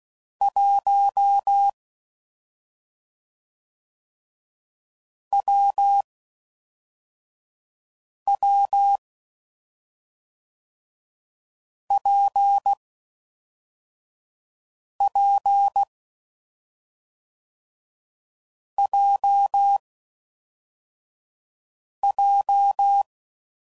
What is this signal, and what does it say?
Morse code, heard as a keyed tone, about 16 wpm, 800 Hz. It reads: 1 W W P P J J
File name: Combinations / Difficult Combos / AWJ1